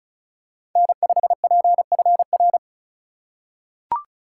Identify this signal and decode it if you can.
Morse code, heard as a keyed tone, about 36 wpm, 700 Hz. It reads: N5PFR
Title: USA Call Signs - 5 Characters - 2x Character Spacing - 35wpm